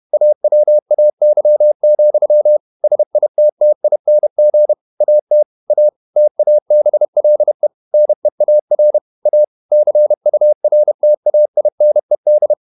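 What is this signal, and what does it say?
Morse code, heard as a keyed tone, about 31 wpm, 600 Hz. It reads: AWAY, SITTING AT A TABLE NEAR A CURTAINED
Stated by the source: Morse code at 30 wpm, the novel Never by Ken Follett in English, from CW med Gnister.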